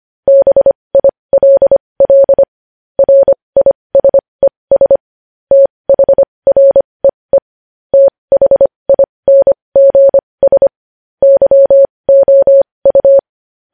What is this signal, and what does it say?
Morse code, heard as a keyed tone, about 25 wpm, spaced as for 20 wpm, 550 Hz. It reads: BILL RISES THREE THINGS YOU
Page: news headlines in morse code at 20 wpm